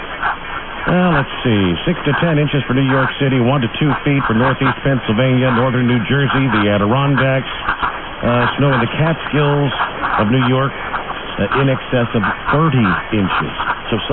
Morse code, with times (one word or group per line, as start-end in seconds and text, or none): none